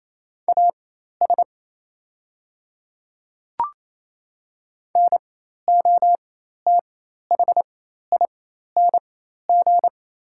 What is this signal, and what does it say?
Morse code, heard as a keyed tone, about 27 wpm, spaced as for 13 wpm, 700 Hz. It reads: AS NOTHING
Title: 4x Character Spacing - Top 500 Words - 28wpm